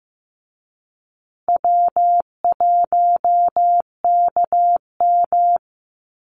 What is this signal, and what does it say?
Morse code, heard as a keyed tone, about 15 wpm, 700 Hz. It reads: W1KM
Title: Easy Call Signs Repeated 2x - 15wpm